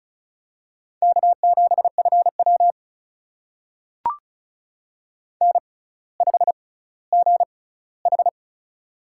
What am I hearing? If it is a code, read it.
Morse code, heard as a keyed tone, about 36 wpm, spaced as for 18 wpm, 700 Hz. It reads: K7FW N5GH
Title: USA Call Signs - 4 Characters - 6x Character Spacing - 35wpm